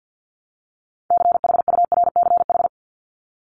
Morse code, heard as a keyed tone, about 50 wpm, 700 Hz. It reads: CHURCH
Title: Top 1751-2000 Words 50wpm